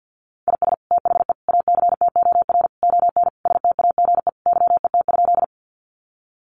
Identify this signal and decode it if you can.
Morse code, heard as a keyed tone, about 50 wpm, 700 Hz. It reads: IS THE ACTOR ON STAGE YET?
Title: Sentences from Top 1500 Words 50wpm